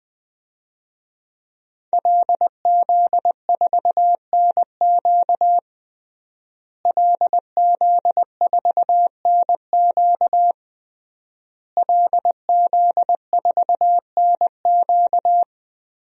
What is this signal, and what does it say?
Morse code, heard as a keyed tone, about 20 wpm, 700 Hz. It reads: LZ4NQ LZ4NQ LZ4NQ
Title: DXCC-100 Call Signs - 5 Characters - Repeated 3x - 20wpm